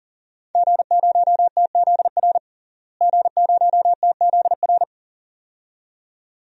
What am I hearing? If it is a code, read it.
Morse code, heard as a keyed tone, about 40 wpm, 700 Hz. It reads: G0TZR G0TZR
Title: DXCC-100 Call Signs - 5 Characters - Repeated 2x - 40wpm